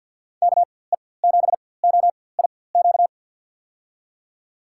Morse code, heard as a keyed tone, about 53 wpm, spaced as for 24 wpm, 700 Hz. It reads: KE7OIQ
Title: USA Call Signs - 4x Character Spacing - 50wpm